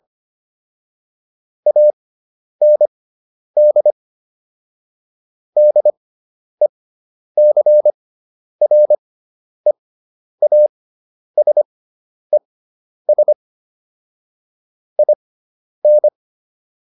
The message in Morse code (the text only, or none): AND DECREASES IN